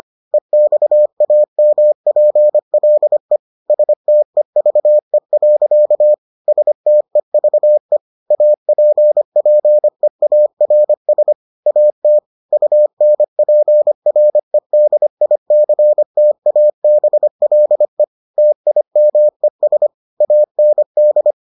Morse code, heard as a keyed tone, about 25 wpm, 600 Hz. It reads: EXAMPLE STEVE. STEVE APPEARS AT UNPREDICTABLE TIMES AND